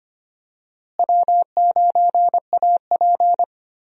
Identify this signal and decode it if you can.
Morse code, heard as a keyed tone, about 25 wpm, 700 Hz. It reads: W9AP
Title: USA Call Signs - 4 Characters - Repeated 3x - 25wpm